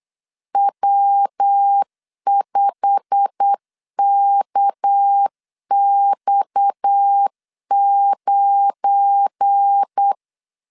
Morse code, heard as a keyed tone, about 8 wpm, 800 Hz. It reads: W5KX9
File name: LEZIONI / 8 YJ45 / 8 YJ45 8wpm 3214